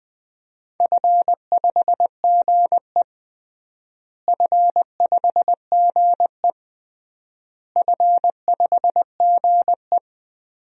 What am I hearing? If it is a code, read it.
Morse code, heard as a keyed tone, about 20 wpm, 700 Hz. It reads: F5GE F5GE F5GE
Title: DXCC-100 Call Signs - 4 Characters - Repeated 3x - 20wpm